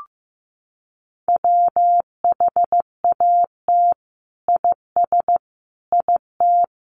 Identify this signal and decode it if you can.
Morse code, heard as a keyed tone, about 15 wpm, 700 Hz. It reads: WHAT IS IT
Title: Sentences from Top 100 Words 15wpm